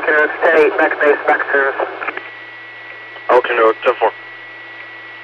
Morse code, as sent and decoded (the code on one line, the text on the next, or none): none